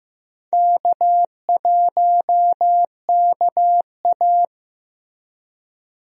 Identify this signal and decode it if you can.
Morse code, heard as a keyed tone, about 16 wpm, 700 Hz. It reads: K1KA
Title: USA Call Signs - 4 Characters - Repeated 2x - 15wpm